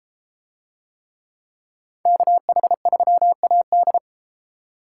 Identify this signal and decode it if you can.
Morse code, heard as a keyed tone, about 32 wpm, 700 Hz. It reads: KH3AD